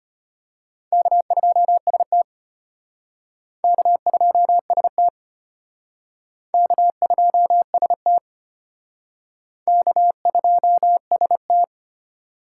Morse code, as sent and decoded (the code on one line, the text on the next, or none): -.- ..--- ... - / -.- ..--- ... - / -.- ..--- ... - / -.- ..--- ... -
K2ST K2ST K2ST K2ST